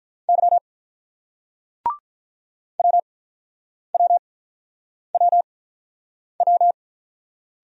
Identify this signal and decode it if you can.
Morse code, heard as a keyed tone, about 48 wpm, 700 Hz. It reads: Y W W W W